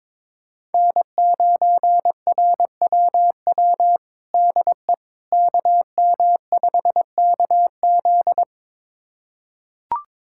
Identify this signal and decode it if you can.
Morse code, heard as a keyed tone, about 22 wpm, 700 Hz. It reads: N9RWW DE KM5KZ
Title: USA Callsign DE Callsign - 5 Characters - 22wpm